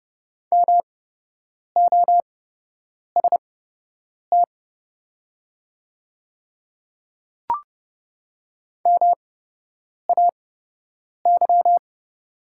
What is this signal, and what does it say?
Morse code, heard as a keyed tone, about 29 wpm, spaced as for 8 wpm, 700 Hz. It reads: MOST MAY
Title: 8x Character Spacing - Top 100 Words - 30wpm